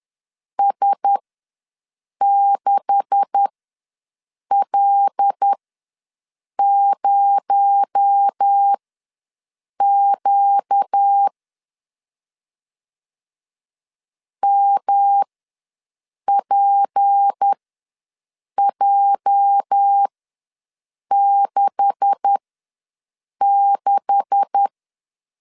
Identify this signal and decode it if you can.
Morse code, heard as a keyed tone, about 10 wpm, spaced as for 6 wpm, 800 Hz. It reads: S6L0Q MPJ66
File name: LEZIONI / 9 286 ws 10wpm 4032